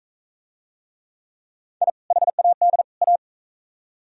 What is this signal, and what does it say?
Morse code, heard as a keyed tone, about 38 wpm, 700 Hz. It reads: I HAD A